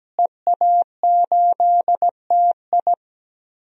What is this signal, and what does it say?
Morse code, heard as a keyed tone, about 17 wpm, 700 Hz. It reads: EA8TI
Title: DXCC-100 Call Signs - 5 Characters - Repeated 2x - 17wpm